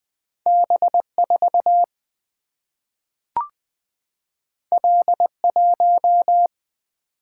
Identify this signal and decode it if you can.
Morse code, heard as a keyed tone, about 20 wpm, 700 Hz. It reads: B 4 L 1
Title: Single-Letter Single-Number 20wpm